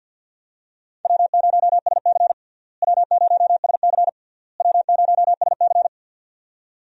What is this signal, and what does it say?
Morse code, heard as a keyed tone, about 51 wpm, 700 Hz. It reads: W0SC W0SC W0SC
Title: USA Call Signs - 4 Characters - Repeated 3x - 50wpm